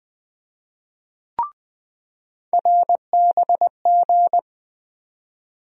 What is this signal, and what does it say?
Morse code, heard as a keyed tone, about 21 wpm, 700 Hz. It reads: RBG
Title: USA Call Sign Postfixes - 3 Letters 20wpm